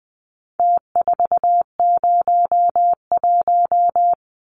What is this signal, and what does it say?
Morse code, heard as a keyed tone, about 20 wpm, 700 Hz. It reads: T401
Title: Numbers - Contest Style - 4-Digit 20wpm